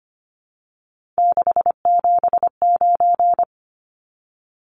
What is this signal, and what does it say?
Morse code, heard as a keyed tone, about 25 wpm, 700 Hz. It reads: 679
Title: Numbers - 3-Digit 25wpm